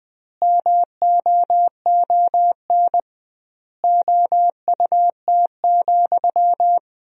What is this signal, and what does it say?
Morse code, heard as a keyed tone, about 21 wpm, 700 Hz. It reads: MOON OUT,
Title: 2x Word-Spacing - Sentences from Top 500 Words - 20wpm